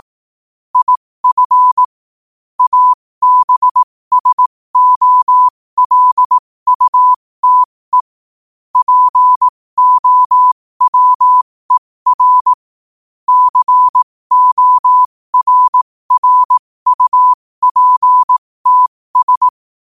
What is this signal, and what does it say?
Morse code, heard as a keyed tone, about 18 wpm, spaced as for 15 wpm, 1000 Hz. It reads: IF ABSOLUTE POWER CORRUPTS
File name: qotdfiles / 15wpm